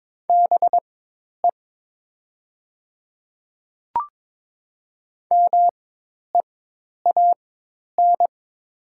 Morse code, heard as a keyed tone, about 22 wpm, spaced as for 10 wpm, 700 Hz. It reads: BE MEAN
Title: 4x Character Spacing - Top 200 Words - 22wpm